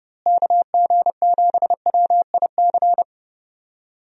Morse code, heard as a keyed tone, about 30 wpm, 700 Hz. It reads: KG7WIC
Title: USA Call Signs - 6 Characters - Repeated 2x - 30wpm